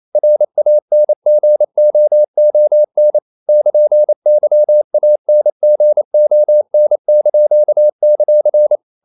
Morse code, heard as a keyed tone, about 28 wpm, 600 Hz. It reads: RANGOON <KN>YANGON);
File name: CW / Practice / NationalCapitals28